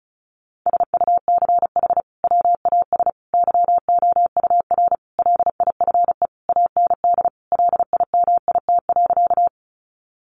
Morse code, heard as a keyed tone, about 34 wpm, 700 Hz. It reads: SUCH WAS YOUR LIFE AND LIMIT.
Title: Sentences from Top 1500 Words 35wpm